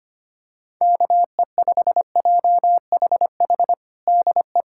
Morse code, heard as a keyed tone, about 25 wpm, 700 Hz. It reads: KE5JHH DE
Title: USA Callsign DE Callsign - 6 Characters - 25wpm